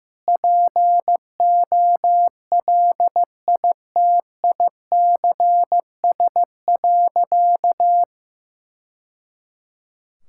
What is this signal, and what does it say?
Morse code, heard as a keyed tone, about 15 wpm, 700 Hz. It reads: POLITICS.
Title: Sentences from Top 1000 Words - Review Part 6 - 15wpm